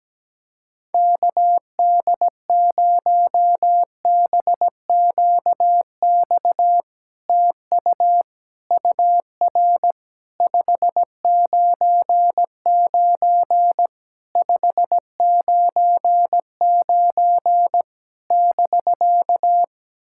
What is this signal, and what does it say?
Morse code, heard as a keyed tone, about 17 wpm, 700 Hz. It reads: KD0BQX TU UR 599 599 <BK>
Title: SOTA QSO Practice 17wpm